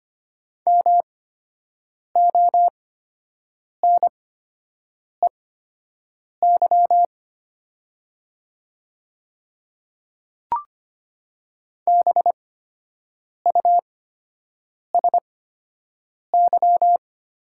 Morse code, heard as a keyed tone, about 25 wpm, spaced as for 7 wpm, 700 Hz. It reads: MONEY BUSY
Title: 8x Character Spacing - Top 500 Words - 25wpm